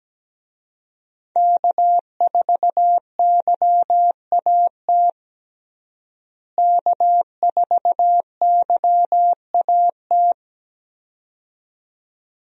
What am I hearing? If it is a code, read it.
Morse code, heard as a keyed tone, about 17 wpm, 700 Hz. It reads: K4YAT K4YAT
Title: USA Call Signs - 5 Characters - Repeated 2x - 17wpm